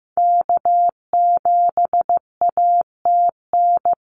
Morse code, heard as a keyed tone, about 15 wpm, 700 Hz. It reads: K7ATN
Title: NA SOTA Activator Callsigns 15wpm